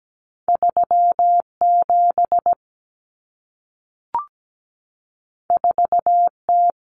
Morse code, heard as a keyed tone, about 17 wpm, 700 Hz. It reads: 37 4T
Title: Common Cut Numbers - 2-Digit 17wpm